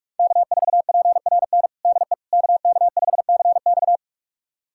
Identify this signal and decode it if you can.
Morse code, heard as a keyed tone, about 44 wpm, 700 Hz. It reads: K4PRN DE KK5CX
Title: USA Callsign DE Callsign - 5 Characters - 45wpm